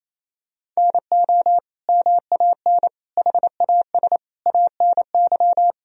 Morse code, heard as a keyed tone, about 28 wpm, 700 Hz. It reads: NO MAN HAS ANY
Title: Sentences from Top 1000 Words - Review Part 2 - 28wpm